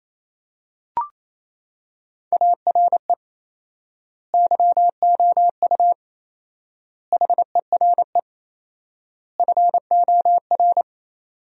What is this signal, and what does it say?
Morse code, heard as a keyed tone, about 28 wpm, 700 Hz. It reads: ARE YOU HERE FOR
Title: 4x Word-Spacing - Sentences from Top 200 Words - 28wpm